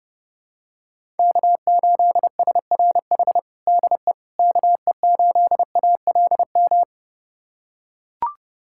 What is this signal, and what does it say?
Morse code, heard as a keyed tone, about 30 wpm, 700 Hz. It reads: K8SRH DE KE8ALM
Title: USA Callsign DE Callsign 30wpm